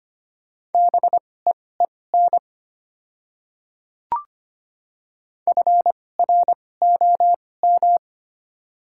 Morse code, heard as a keyed tone, about 25 wpm, spaced as for 18 wpm, 690 Hz. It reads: BEEN FROM
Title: 2x Character Spacing - Top 200 Words - 25wpm